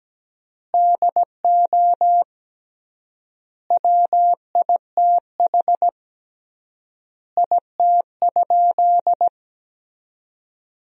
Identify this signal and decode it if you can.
Morse code, heard as a keyed tone, about 17 wpm, 700 Hz. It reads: DO WITH IT?
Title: 3x Word-Spacing - Sentences from Top 100 Words - 17wpm